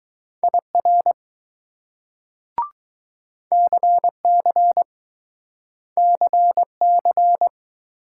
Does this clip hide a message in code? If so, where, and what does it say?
Yes: Morse code from 0.4 to 7.5 s, heard as a keyed tone, about 20 wpm, 700 Hz. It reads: IR CC CC